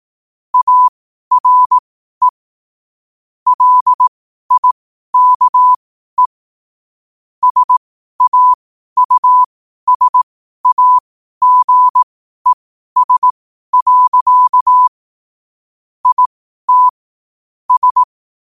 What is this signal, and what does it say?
Morse code, heard as a keyed tone, about 18 wpm, spaced as for 12 wpm, 1000 Hz. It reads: ARE LIKE SAUSAGES. IT S
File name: qotdfiles / 12wpm